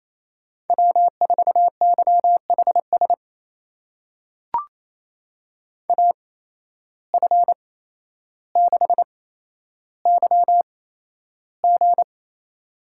Morse code, heard as a keyed tone, about 28 wpm, spaced as for 12 wpm, 700 Hz. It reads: W4YHS AF6YG
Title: USA Call Signs - 5 Characters - 8x Character Spacing - 28wpm